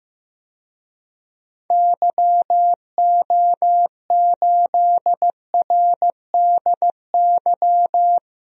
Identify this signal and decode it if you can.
Morse code, heard as a keyed tone, about 15 wpm, 700 Hz. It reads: YO8RDY
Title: DXCC-100 Call Signs - 6 Characters - Repeated 3x - 15wpm